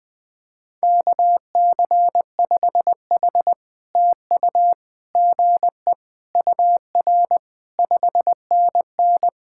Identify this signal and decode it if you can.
Morse code, heard as a keyed tone, about 20 wpm, 700 Hz. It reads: KC5H TU GE UR 5NN
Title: POTA QSO Practice - 20wpm